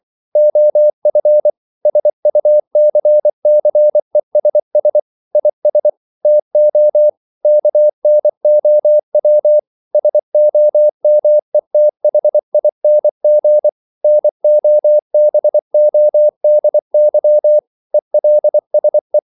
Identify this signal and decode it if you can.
Morse code, heard as a keyed tone, about 24 wpm, 600 Hz. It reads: OF SUCCESS IS TO KNOW SOMETHING NOBODY ELSE
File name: CW / Practice / EnglishQuotations0224